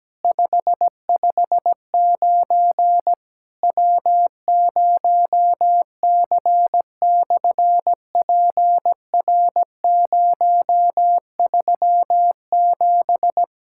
Morse code, heard as a keyed tone, about 17 wpm, 700 Hz. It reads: 559 W0C/PR037
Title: SOTA QSO Practice 17wpm